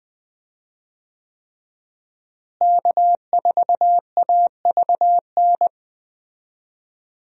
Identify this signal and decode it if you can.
Morse code, heard as a keyed tone, about 20 wpm, 700 Hz. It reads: K4AVN
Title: USA Call Signs - 5 Characters - Repeated 3x - 20wpm